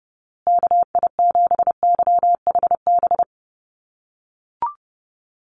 Morse code, heard as a keyed tone, about 30 wpm, 700 Hz. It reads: KI7YHB
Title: USA Call Signs Repeated 3x 30wpm